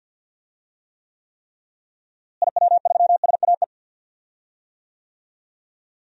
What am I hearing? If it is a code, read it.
Morse code, heard as a keyed tone, about 46 wpm, 700 Hz. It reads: IW3SAE
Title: DXCC-100 Call Signs - 6 Characters - Repeated 3x - 50wpm